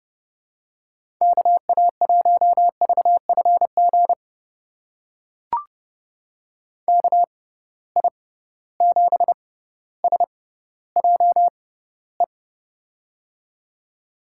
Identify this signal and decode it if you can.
Morse code, heard as a keyed tone, about 30 wpm, spaced as for 15 wpm, 700 Hz. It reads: KA1VFG KI7SJE
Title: USA Call Signs - 6x Character Spacing - 30wpm